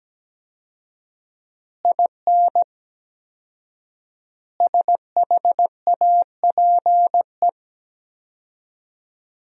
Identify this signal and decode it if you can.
Morse code, heard as a keyed tone, about 17 wpm, 700 Hz. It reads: IN SHAPE
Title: 4x Word-Spacing - Sentences from Top 500 Words - 17wpm